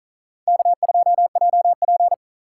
Morse code, heard as a keyed tone, about 46 wpm, 700 Hz. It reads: K2JP